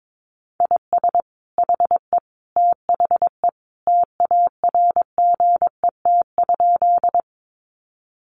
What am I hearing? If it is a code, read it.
Morse code, heard as a keyed tone, about 22 wpm, 700 Hz. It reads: IS HE THE TARGET?
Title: Sentences from Top 1200 Words 22wpm